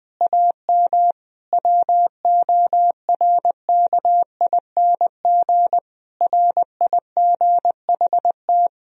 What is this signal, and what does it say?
Morse code, heard as a keyed tone, about 20 wpm, 700 Hz. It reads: AM WORKING RIGHT
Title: Sentences from Top 1000 Words - Review Part 4 - 20wpm